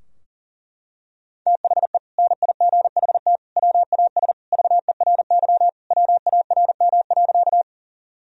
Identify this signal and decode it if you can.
Morse code, heard as a keyed tone, about 40 wpm, 700 Hz. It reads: THE NIGHT WAS VERY WARM.
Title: Sentences from Top 1000 Words - Review Part 2 - 40wpm